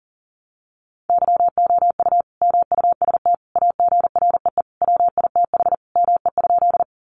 Morse code, heard as a keyed tone, about 40 wpm, 700 Hz. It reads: YOU MUST AGREE WITH ME?
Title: Sentences from Top 1100 Words 40wpm